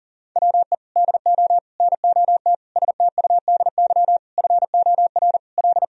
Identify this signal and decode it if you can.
Morse code, heard as a keyed tone, about 40 wpm, 700 Hz. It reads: WE DO NOT STUDY FOR L
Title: Sentences from Top 1000 Words - Review Part 1 - 40wpm